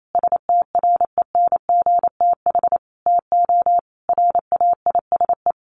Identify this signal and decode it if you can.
Morse code, heard as a keyed tone, about 28 wpm, 700 Hz. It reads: STRENGTH TO RAISE
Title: Sentences from Top 1400 Words 28wpm